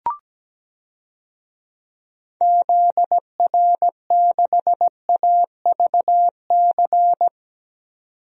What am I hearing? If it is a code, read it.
Morse code, heard as a keyed tone, about 17 wpm, 700 Hz. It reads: ZR6AVC
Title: DXCC-100 Call Signs - 6 Characters - 17wpm